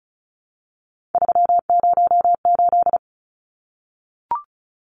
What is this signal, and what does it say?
Morse code, heard as a keyed tone, about 34 wpm, 700 Hz. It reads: 308
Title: Numbers - 3-Digit 35wpm